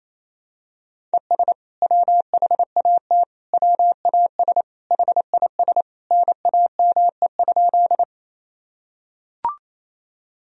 Morse code, heard as a keyed tone, about 28 wpm, 700 Hz. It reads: ES WHAT WAS HIS NAME?
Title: Sentences from Top 200 Words and Abbreviations 28wpm